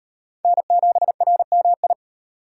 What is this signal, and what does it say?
Morse code, heard as a keyed tone, about 38 wpm, 700 Hz. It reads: N7RMI